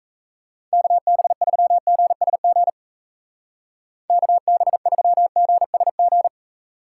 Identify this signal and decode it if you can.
Morse code, heard as a keyed tone, about 34 wpm, 700 Hz. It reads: KB3GSG KB3GSG
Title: Speed Racing - USA 2x3 Call Signs 28wpm